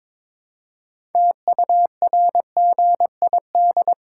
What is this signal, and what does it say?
Morse code, heard as a keyed tone, about 22 wpm, 700 Hz. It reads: TURGID
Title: Pangram Challenge v2 - 22wpm